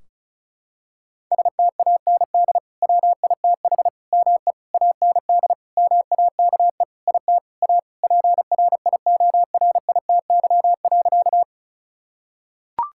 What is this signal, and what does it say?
Morse code, heard as a keyed tone, about 34 wpm, 700 Hz. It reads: STAND WITH ME AND MAKE IT A PRIORITY.
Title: Sentences from Top 1600 Words 35wpm